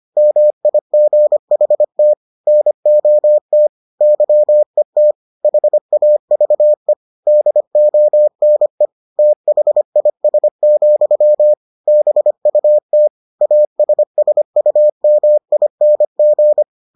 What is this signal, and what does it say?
Morse code, heard as a keyed tone, about 25 wpm, 600 Hz. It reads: MIGHT NOT YET HAVE DONE THIS, BUT ASSUMING